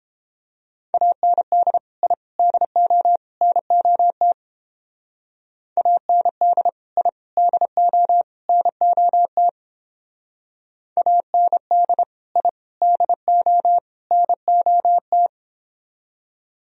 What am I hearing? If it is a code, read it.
Morse code, heard as a keyed tone, about 31 wpm, 700 Hz. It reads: AND I DO NOT AND I DO NOT AND I DO NOT